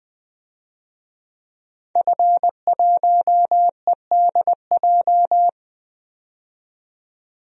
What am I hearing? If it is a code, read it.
Morse code, heard as a keyed tone, about 20 wpm, 700 Hz. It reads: F1EDJ
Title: DXCC-100 Call Signs - 5 Characters - 20wpm